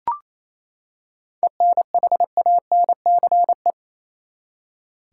Morse code, heard as a keyed tone, about 28 wpm, 700 Hz. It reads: ENHANCE
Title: Seven Letter Words - 28wpm